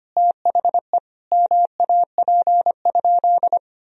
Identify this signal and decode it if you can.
Morse code, heard as a keyed tone, about 25 wpm, 700 Hz. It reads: THE MAP?
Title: Sentences from Top 1000 Words - Review Part 2 - 25wpm